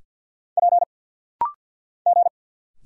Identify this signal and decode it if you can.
Morse code, heard as a keyed tone, about 50 wpm, 700 Hz. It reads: P G
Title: Single Letters - Rapid-Fire 50wpm